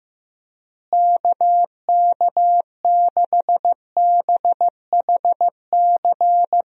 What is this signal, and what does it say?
Morse code, heard as a keyed tone, about 16 wpm, 700 Hz. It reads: KK6BHC
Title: USA Call Signs - 6 Characters - Repeated 2x - 15wpm